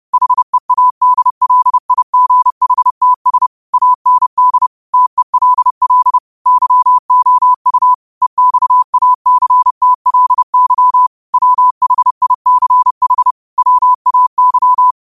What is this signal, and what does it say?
Morse code, heard as a keyed tone, about 30 wpm, 1000 Hz. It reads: HEADLIGHTS AND TELL YOU EXACTLY WHICH WAY